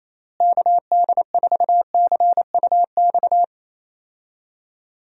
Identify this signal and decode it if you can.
Morse code, heard as a keyed tone, about 28 wpm, 700 Hz. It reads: KD4CUX
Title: USA Call Signs - 6 Characters - Repeated 2x - 28wpm